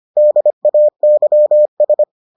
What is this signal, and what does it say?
Morse code, heard as a keyed tone, about 25 wpm, 600 Hz. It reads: DAYS